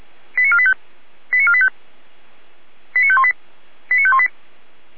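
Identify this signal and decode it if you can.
Five-tone selective call (ZVEI), heard as quick runs of five tones, tones of about 70 ms. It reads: 89485 89485 89518 89518
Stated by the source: ZVEI five-tone selective call received on SDR.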